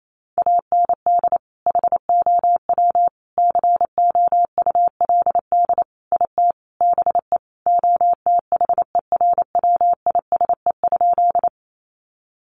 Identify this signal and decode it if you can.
Morse code, heard as a keyed tone, about 28 wpm, 700 Hz. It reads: AND HOW COULD IT BE OTHERWISE?
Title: Sentences from Top 1200 Words 28wpm